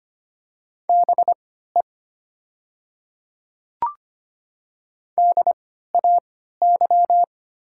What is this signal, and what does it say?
Morse code, heard as a keyed tone, about 25 wpm, spaced as for 14 wpm, 700 Hz. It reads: BE DAY
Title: 3x Character Spacing - Top 100 Words - 25wpm